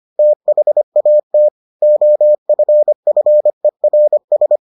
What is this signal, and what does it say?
Morse code, heard as a keyed tone, about 25 wpm, 600 Hz. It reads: THAT OFFERS